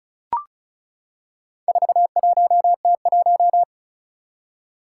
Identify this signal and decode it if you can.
Morse code, heard as a keyed tone, about 36 wpm, 700 Hz. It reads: 41T1
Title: Four Challenging Letter-Number Combinations 35wpm